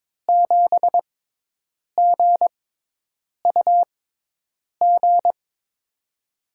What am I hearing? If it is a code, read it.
Morse code, heard as a keyed tone, about 22 wpm, spaced as for 8 wpm, 700 Hz. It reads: 7GUG
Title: USA Call Signs - 6x Character Spacing - 22wpm